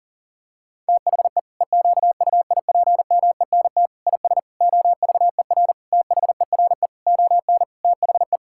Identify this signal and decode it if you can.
Morse code, heard as a keyed tone, about 40 wpm, 700 Hz. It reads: THE EQUIPMENT IS OVER THERE ON THE
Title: Sentences from Top 1000 Words - Review Part 5 - 40wpm